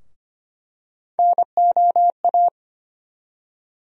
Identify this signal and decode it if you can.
Morse code, heard as a keyed tone, about 25 wpm, 700 Hz. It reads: NOA
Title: USA Call Sign Postfixes - 3 Letters 25wpm